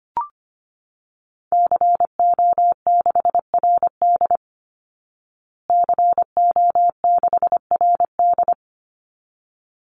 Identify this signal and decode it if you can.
Morse code, heard as a keyed tone, about 25 wpm, 700 Hz. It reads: CO6RD CO6RD
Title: Call Signs Repeated 3x - 25wpm